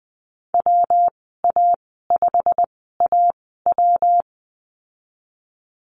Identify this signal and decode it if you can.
Morse code, heard as a keyed tone, about 20 wpm, spaced as for 15 wpm, 700 Hz. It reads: WA5AW
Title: 167 Lesson 27g - Number 7 - Call Signs 20wpm